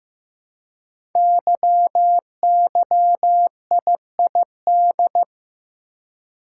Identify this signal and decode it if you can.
Morse code, heard as a keyed tone, about 15 wpm, 700 Hz. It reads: YYIID
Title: Code Groups - 5 Letter:Numbers - 15wpm